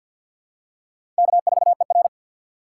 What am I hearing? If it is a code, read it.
Morse code, heard as a keyed tone, about 50 wpm, 700 Hz. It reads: K4ER
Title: USA Call Signs - 4 Characters - 3x Character Spacing - 50wpm